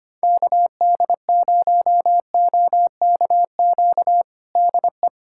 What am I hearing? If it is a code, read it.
Morse code, heard as a keyed tone, about 25 wpm, 700 Hz. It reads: KD0OKQ DE
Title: USA Callsign DE Callsign - 6 Characters - 25wpm